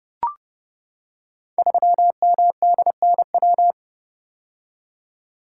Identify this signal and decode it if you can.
Morse code, heard as a keyed tone, about 30 wpm, 700 Hz. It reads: 3MDNW
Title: Code Groups - 5 Letter:Numbers - 30wpm